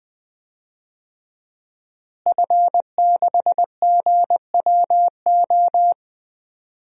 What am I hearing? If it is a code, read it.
Morse code, heard as a keyed tone, about 20 wpm, 700 Hz. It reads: F6GWO
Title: DXCC-100 Call Signs - 5 Characters - Repeated 3x - 20wpm